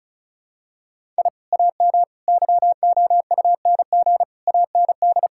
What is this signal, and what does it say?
Morse code, heard as a keyed tone, about 36 wpm, 700 Hz. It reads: I AM YOUNG AND
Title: Sentences from Top 1000 Words - Review Part 5 - 35wpm